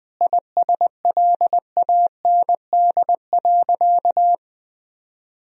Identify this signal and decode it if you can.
Morse code, heard as a keyed tone, about 20 wpm, 700 Hz. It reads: ISLAND.
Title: Sentences from Top 1000 Words - Review Part 3 - 20wpm